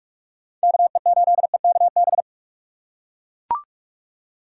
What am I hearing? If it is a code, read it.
Morse code, heard as a keyed tone, about 43 wpm, 700 Hz. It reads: KE8EKB